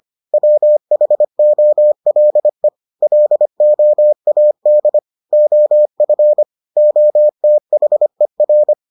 Morse code, heard as a keyed tone, about 25 wpm, 600 Hz. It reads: WHOLE LOAD OF OTHER